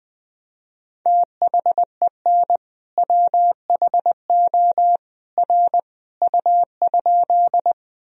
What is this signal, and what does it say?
Morse code, heard as a keyed tone, about 20 wpm, 700 Hz. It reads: THEN WHO R U?
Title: Sentences from Top 100 Words and Abbreviations 20wpm